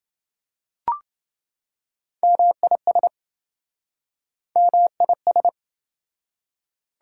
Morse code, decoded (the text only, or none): MIS MIS